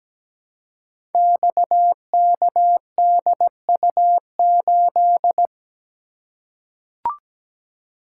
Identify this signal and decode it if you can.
Morse code, heard as a keyed tone, about 17 wpm, 700 Hz. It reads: XKDU8
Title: Code Groups - 5 Letter:Numbers - 17wpm